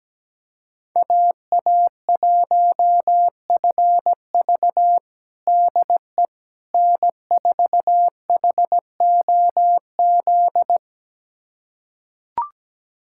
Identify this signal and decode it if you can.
Morse code, heard as a keyed tone, about 17 wpm, 700 Hz. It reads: AA1FV DE N4HOZ
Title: USA Callsign DE Callsign - 5 Characters - 17wpm